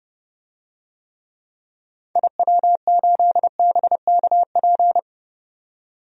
Morse code, heard as a keyed tone, about 31 wpm, 700 Hz. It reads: IW8BKP
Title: DXCC-100 Call Signs - 6 Characters - 30wpm